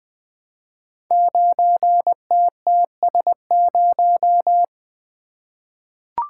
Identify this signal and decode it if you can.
Morse code, heard as a keyed tone, about 20 wpm, 700 Hz. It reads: 9TTS0
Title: Code Groups - 5 Letter:Numbers - 20wpm